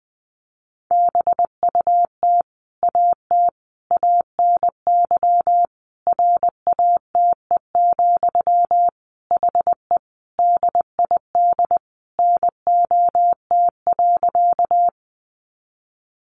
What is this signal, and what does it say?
Morse code, heard as a keyed tone, about 20 wpm, 700 Hz. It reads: BUT AT ANY RATE, HE DID NOT.
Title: Sentences from Top 700 Words 20wpm